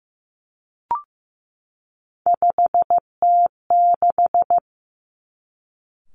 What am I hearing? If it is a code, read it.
Morse code, heard as a keyed tone, about 15 wpm, 700 Hz. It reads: 5T6
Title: Three Challenging Letter-Number Combinations 15wpm